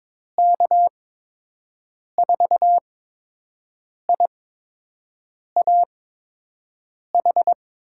Morse code, heard as a keyed tone, about 22 wpm, spaced as for 6 wpm, 700 Hz. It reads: K4IAH
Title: USA Call Signs - 6 Characters - 8x Character Spacing - 22wpm